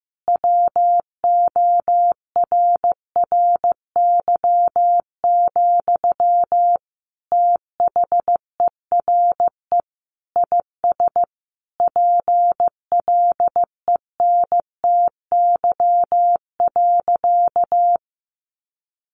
Morse code, heard as a keyed tone, about 15 wpm, 700 Hz. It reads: WORRY, THERE IS PLENTY.
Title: Sentences from Top 1300 Words 15wpm